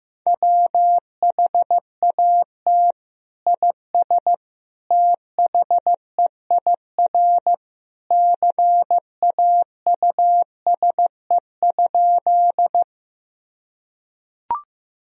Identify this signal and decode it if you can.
Morse code, heard as a keyed tone, about 16 wpm, 700 Hz. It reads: WHAT IS THEIR CAUSE?
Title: Sentences from Top 1000 Words - Review Part 1 - 15wpm